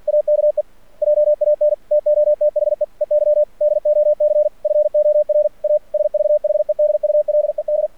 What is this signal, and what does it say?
Morse code, heard as a keyed tone, about 45 wpm, 600 Hz. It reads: AKE JAN TO THE EYDOC FOR A SVHEDULED
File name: archived / CRSnet